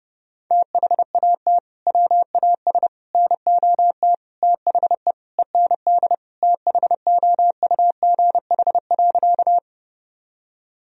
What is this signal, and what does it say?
Morse code, heard as a keyed tone, about 30 wpm, 700 Hz. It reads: THAT WAS NOT THE END THOUGH.
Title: Sentences from Top 1000 Words - Review Part 2 - 30wpm